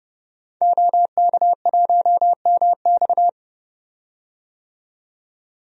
Morse code, heard as a keyed tone, about 30 wpm, 700 Hz. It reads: OK1MX
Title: DXCC-100 Call Signs - 5 Characters - 30wpm